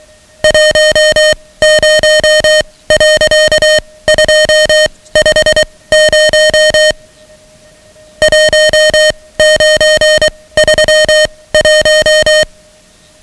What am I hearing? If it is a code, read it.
Morse code, heard as a keyed tone, about 23 wpm, spaced as for 17 wpm, 620 Hz, with noise. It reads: 10.250 1931
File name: ham radio / CodeQuick / Code Quick 10